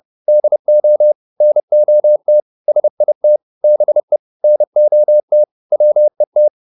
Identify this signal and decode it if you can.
Morse code, heard as a keyed tone, about 30 wpm, 600 Hz. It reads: DO NOT SIT BE NOT WET